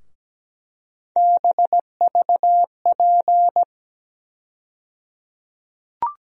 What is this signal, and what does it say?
Morse code, heard as a keyed tone, about 17 wpm, 700 Hz. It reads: BVP
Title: USA Call Sign Postfixes - 3 Letters 17wpm